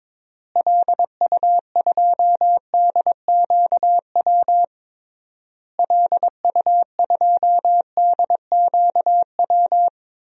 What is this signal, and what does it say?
Morse code, heard as a keyed tone, about 22 wpm, 700 Hz. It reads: LU2DQW LU2DQW
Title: DXCC-100 Call Signs - 6 Characters - Repeated 3x - 22wpm